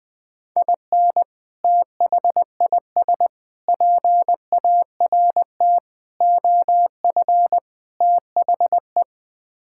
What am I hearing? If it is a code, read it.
Morse code, heard as a keyed tone, about 20 wpm, 700 Hz. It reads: IN THIS PART OF THE
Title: Build A Phrase - 5-Word Phrases - 20wpm